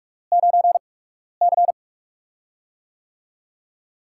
Morse code, heard as a keyed tone, about 44 wpm, spaced as for 12 wpm, 700 Hz. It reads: 9C
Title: USA Call Signs - 8x Character Spacing - 45wpm